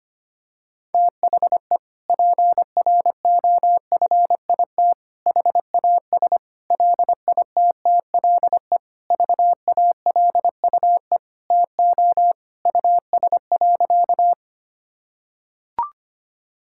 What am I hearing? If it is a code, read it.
Morse code, heard as a keyed tone, about 25 wpm, 700 Hz. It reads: THE PROFIT HAS LITTLE VALUE TO US.
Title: Sentences from Top 1000 Words - Review Part 5 - 25wpm